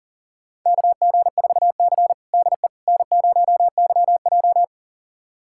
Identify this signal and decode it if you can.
Morse code, heard as a keyed tone, about 37 wpm, 700 Hz. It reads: KG4C DE N0YJ
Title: USA Callsign DE Callsign - 4 Characters - 40wpm